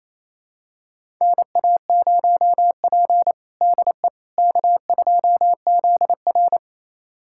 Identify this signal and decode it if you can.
Morse code, heard as a keyed tone, about 28 wpm, 700 Hz. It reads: NA0P DE K2ZR
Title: USA Callsign DE Callsign - 4 Characters - 28wpm